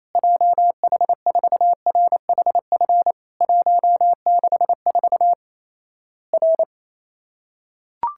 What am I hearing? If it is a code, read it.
Morse code, heard as a keyed tone, about 28 wpm, 700 Hz. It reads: JH4RHF 164 R
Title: Worked All Europe DX Contest - QTC Exchange - 28wpm